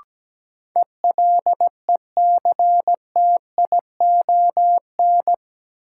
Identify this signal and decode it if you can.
Morse code, heard as a keyed tone, about 17 wpm, 700 Hz. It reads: ELECTION
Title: Sentences from Top 1500 Words - New Word Drill Alt - 17wpm